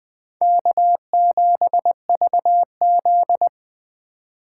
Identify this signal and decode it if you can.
Morse code, heard as a keyed tone, about 24 wpm, 700 Hz. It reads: K7VZ
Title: Speed Racing - USA 1x2 Call Signs 20wpm